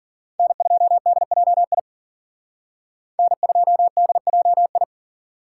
Morse code, heard as a keyed tone, about 51 wpm, 700 Hz. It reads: N2DJI N2DJI